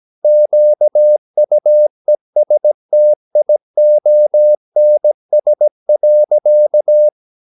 Morse code, heard as a keyed tone, about 17 wpm, 600 Hz. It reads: QUESTIONS.
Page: Romanen Never af Ken Follet på engelsk.